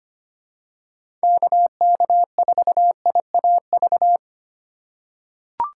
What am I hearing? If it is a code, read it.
Morse code, heard as a keyed tone, about 25 wpm, 700 Hz. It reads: KK4IAV
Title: USA Call Signs - 6 Characters - 4x Character Spacing - 25wpm